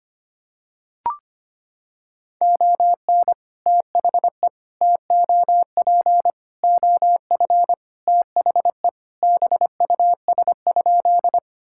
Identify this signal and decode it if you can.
Morse code, heard as a keyed tone, about 25 wpm, 700 Hz. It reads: ON THE TOP OF THE BUS?
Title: Sentences from Top 1000 Words - Review Part 6 - 25wpm